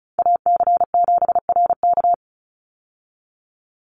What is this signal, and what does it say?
Morse code, heard as a keyed tone, about 34 wpm, 700 Hz. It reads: AC7RK
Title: USA Call Signs 35wpm